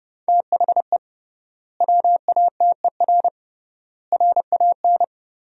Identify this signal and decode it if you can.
Morse code, heard as a keyed tone, about 30 wpm, 700 Hz. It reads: THE WATER RAN
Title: 3x Word-Spacing - Sentences from Top 500 Words - 30wpm